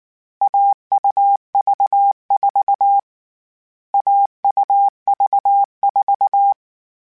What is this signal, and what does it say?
Morse code, heard as a keyed tone, about 19 wpm, 800 Hz. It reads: AUV4 AUV4